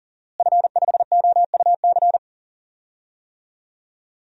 Morse code, heard as a keyed tone, about 40 wpm, 700 Hz. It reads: F5OUC
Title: DXCC-100 Call Signs - 5 Characters - Repeated 2x - 40wpm